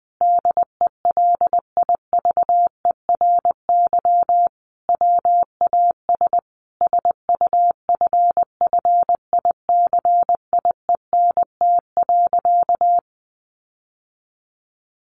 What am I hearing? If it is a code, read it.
Morse code, heard as a keyed tone, about 20 wpm, 700 Hz. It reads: DELIVERY WAS SUFFICIENT.
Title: Sentences from Top 1500 Words 20wpm